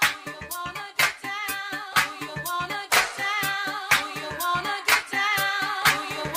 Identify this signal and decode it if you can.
Emergency Alert System (SAME) tones (header bursts), heard as harsh buzzing bursts. It is heard from 0.9 to 5.7 s.